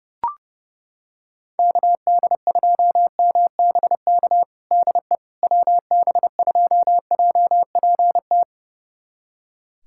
Morse code, heard as a keyed tone, about 30 wpm, 700 Hz. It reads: KD2MBK DE WB2JPT
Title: USA Callsign DE Callsign 30wpm